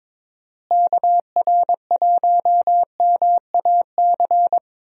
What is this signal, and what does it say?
Morse code, heard as a keyed tone, about 22 wpm, 700 Hz. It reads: KR1MAC
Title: USA Call Signs - 6 Characters - Repeated 3x - 22wpm